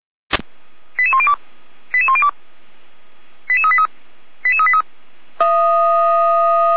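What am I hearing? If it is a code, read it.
Five-tone selective call (ZVEI), heard as quick runs of five tones, tones of about 70 ms. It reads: 88102 88102 88383 88383
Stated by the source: ZVEI five-tone call 88124 received by SDR.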